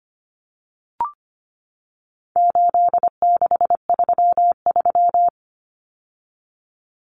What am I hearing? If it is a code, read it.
Morse code, heard as a keyed tone, about 25 wpm, 700 Hz. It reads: 8633
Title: Numbers - 4-Digit 25wpm